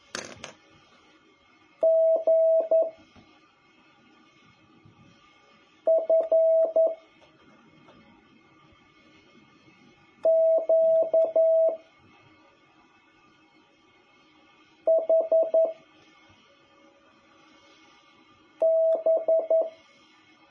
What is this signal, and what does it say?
Morse code, heard as a keyed tone, about 11 wpm, 650 Hz. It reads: G F Q H B
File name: jh8cbh / 2021cw